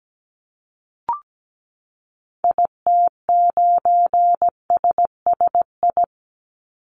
Morse code, heard as a keyed tone, about 17 wpm, 700 Hz. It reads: IT9SSI
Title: Call Signs Repeated 2x - 17wpm